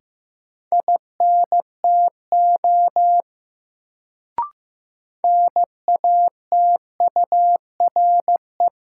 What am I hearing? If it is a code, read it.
Morse code, heard as a keyed tone, about 15 wpm, 700 Hz. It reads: INTO NATURE
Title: Sentences from Top 600 Words - New Word Drill - 15wpm